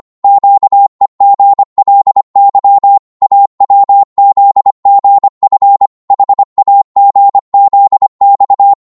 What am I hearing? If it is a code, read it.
Morse code, heard as a keyed tone, about 25 wpm, 800 Hz. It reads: QEGLY AWZGF HAGZX
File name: Morse / ltr-25wpm